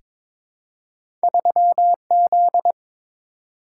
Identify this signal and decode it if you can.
Morse code, heard as a keyed tone, about 23 wpm, 700 Hz. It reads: 3Z
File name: practice / 22wpm / Two Letter-Number Combinations - 22wpm